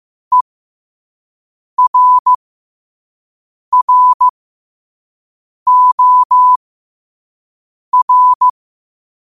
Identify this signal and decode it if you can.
Morse code, heard as a keyed tone, about 15 wpm, spaced as for 5 wpm, 1000 Hz. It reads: ERROR
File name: qotdfiles / 5wpm